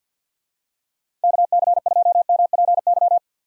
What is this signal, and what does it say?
Morse code, heard as a keyed tone, about 52 wpm, 700 Hz. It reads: KC2MPY
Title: USA Call Signs - 6 Characters - 50wpm